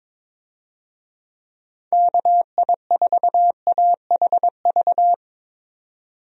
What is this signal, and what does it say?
Morse code, heard as a keyed tone, about 21 wpm, 700 Hz. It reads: KI4AHV
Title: USA Call Signs - 6 Characters - 22wpm